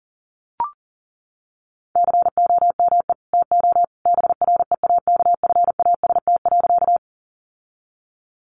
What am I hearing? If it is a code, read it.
Morse code, heard as a keyed tone, about 40 wpm, 700 Hz. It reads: COME TO BREAKFAST.
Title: Sentences from Top 1500 Words 40wpm